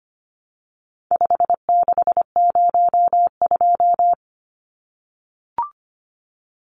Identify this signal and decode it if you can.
Morse code, heard as a keyed tone, about 25 wpm, 700 Hz. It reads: 5602
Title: Numbers - 4-Digit 25wpm